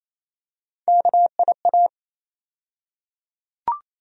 Morse code, heard as a keyed tone, about 28 wpm, 700 Hz. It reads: KIA
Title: USA Call Sign Postfixes - 3 Letters 28wpm